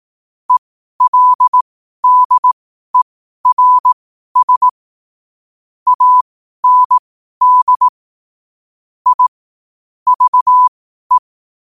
Morse code, heard as a keyed tone, about 18 wpm, spaced as for 12 wpm, 1000 Hz. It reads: ELDERS AND I VE
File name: qotdfiles / 12wpm